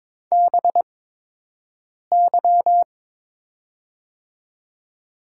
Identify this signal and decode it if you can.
Morse code, heard as a keyed tone, about 21 wpm, spaced as for 6 wpm, 700 Hz. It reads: BY
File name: Top 90 QSO Elements - 8x Character Spacing - 22wpm